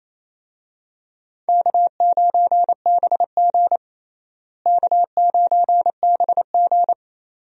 Morse code, heard as a keyed tone, about 28 wpm, 700 Hz. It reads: K9BG K9BG
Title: USA Call Signs - 4 Characters - Repeated 2x - 28wpm